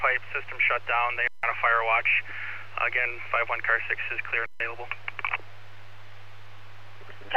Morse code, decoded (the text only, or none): none